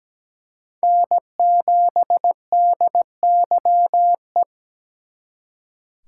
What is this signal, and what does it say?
Morse code, heard as a keyed tone, about 20 wpm, 700 Hz. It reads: N7DYE
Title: Speed Racing - USA 1x3 Call Signs 17wpm